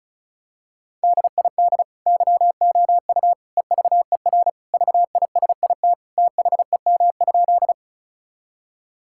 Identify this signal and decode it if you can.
Morse code, heard as a keyed tone, about 36 wpm, 700 Hz. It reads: DID YOU EVER VISIT THEM?
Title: Sentences from Top 1000 Words - Review Part 6 - 35wpm